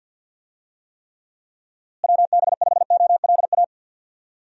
Morse code, heard as a keyed tone, about 53 wpm, 700 Hz. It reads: WB5OLA
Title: USA Call Signs - 6 Characters - Repeated 2x - 50wpm